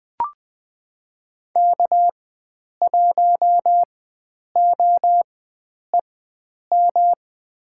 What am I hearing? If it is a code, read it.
Morse code, heard as a keyed tone, about 20 wpm, spaced as for 9 wpm, 700 Hz. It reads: K1OEM
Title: USA Call Signs - 5 Characters - 4x Character Spacing - 20wpm